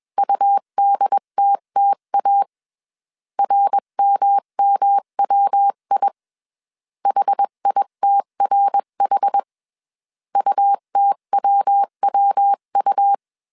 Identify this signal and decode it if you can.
Morse code, heard as a keyed tone, about 21 wpm, 800 Hz. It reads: UDTTA RMMWI HITRH UTWWU
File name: CW / DET / 1 3 1500C S B 20wpm 1754